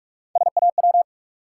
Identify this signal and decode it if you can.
Morse code, heard as a keyed tone, about 46 wpm, 700 Hz. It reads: SAW
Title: Sentences from Top 300 Words - New Word Drill - 45wpm